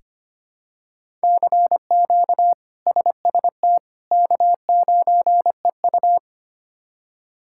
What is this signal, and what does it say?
Morse code, heard as a keyed tone, about 25 wpm, 700 Hz. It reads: CQ SST K9EU
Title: K1USN SST QSO Practice - 25wpm